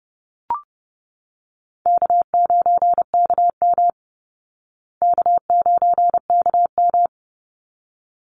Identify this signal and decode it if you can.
Morse code, heard as a keyed tone, about 30 wpm, 700 Hz. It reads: K9KM K9KM
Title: Easy Call Signs Repeated 2x - 30wpm